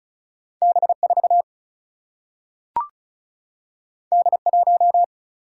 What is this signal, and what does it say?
Morse code, heard as a keyed tone, about 36 wpm, 700 Hz. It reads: B4 D1
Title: Single-Letter Single-Number 35wpm